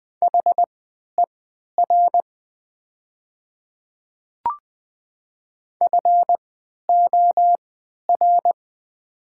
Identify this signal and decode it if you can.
Morse code, heard as a keyed tone, about 20 wpm, spaced as for 11 wpm, 700 Hz. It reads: HER FOR
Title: 3x Character Spacing - Top 100 Words - 20wpm